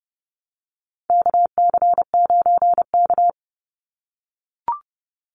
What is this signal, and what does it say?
Morse code, heard as a keyed tone, about 30 wpm, 700 Hz. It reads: KC9K
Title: Call Signs - 30wpm